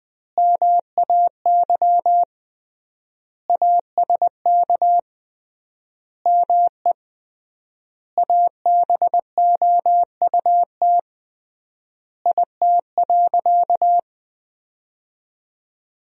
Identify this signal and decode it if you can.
Morse code, heard as a keyed tone, about 20 wpm, 700 Hz. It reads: MAY ASK ME ABOUT IT.
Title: 3x Word-Spacing - Sentences from Top 200 Words - 20wpm